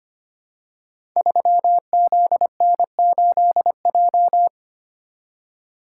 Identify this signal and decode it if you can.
Morse code, heard as a keyed tone, about 25 wpm, 700 Hz. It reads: 3ZN8J
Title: Code Groups - 5 Letter:Numbers - 25wpm